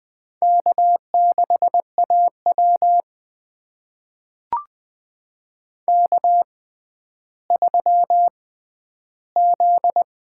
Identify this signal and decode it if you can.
Morse code, heard as a keyed tone, about 20 wpm, spaced as for 11 wpm, 700 Hz. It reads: K6AW K3Z
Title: USA Call Signs - 4 Characters - 6x Character Spacing - 20wpm